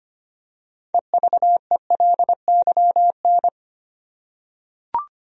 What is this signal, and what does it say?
Morse code, heard as a keyed tone, about 25 wpm, 700 Hz. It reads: EVELYN
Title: USA Common Names - 25wpm